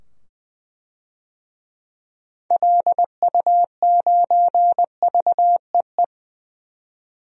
Morse code, heard as a keyed tone, about 20 wpm, 700 Hz. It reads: LU9VEE
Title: DXCC-100 Call Signs - 6 Characters - 20wpm